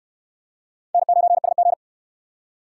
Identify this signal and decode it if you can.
Morse code, heard as a keyed tone, about 63 wpm, 700 Hz. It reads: N9SG